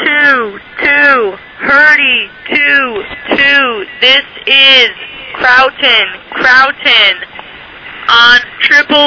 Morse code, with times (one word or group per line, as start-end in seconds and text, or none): none